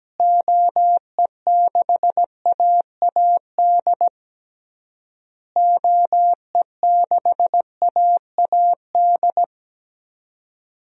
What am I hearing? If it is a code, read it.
Morse code, heard as a keyed tone, about 17 wpm, 700 Hz. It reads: OE6AAD OE6AAD
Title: DXCC-100 Call Signs - 6 Characters - Repeated 3x - 17wpm